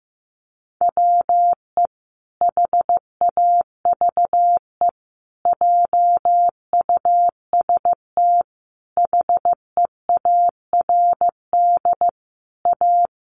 Morse code, heard as a keyed tone, about 15 wpm, 700 Hz. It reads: WE HAVE JUST HEARD A
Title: Sentences from Top 900 Words 15wpm